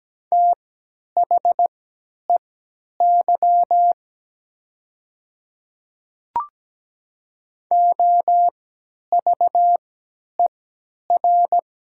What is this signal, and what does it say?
Morse code, heard as a keyed tone, about 17 wpm, spaced as for 10 wpm, 700 Hz. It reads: THEY OVER
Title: 3x Character Spacing - Top 100 Words - 17wpm